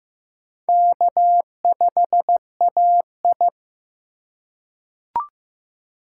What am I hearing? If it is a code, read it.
Morse code, heard as a keyed tone, about 15 wpm, 700 Hz. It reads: K5AI
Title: USA Call Signs - 4 Characters - 2x Character Spacing - 15wpm